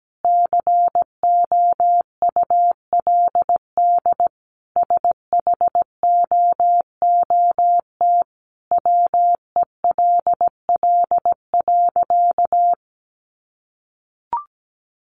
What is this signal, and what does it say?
Morse code, heard as a keyed tone, about 17 wpm, 700 Hz. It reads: COULD SHOOT WELL.
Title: Sentences from Top 1200 Words 17wpm